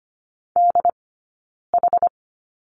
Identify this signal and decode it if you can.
Morse code, heard as a keyed tone, about 24 wpm, 700 Hz. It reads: D H
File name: Lesson 15d - Number 5 - Two Characters 25wpm